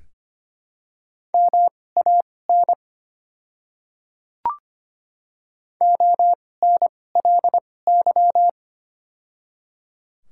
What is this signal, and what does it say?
Morse code, heard as a keyed tone, about 25 wpm, spaced as for 18 wpm, 700 Hz. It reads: MAN ONLY
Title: 2x Character Spacing - Top 300 Words - 25wpm